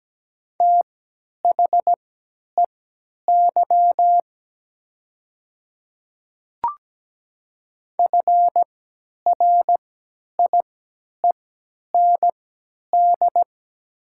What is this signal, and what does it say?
Morse code, heard as a keyed tone, about 17 wpm, spaced as for 10 wpm, 700 Hz. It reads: THEY FRIEND
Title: 3x Character Spacing - Top 300 Words - 17wpm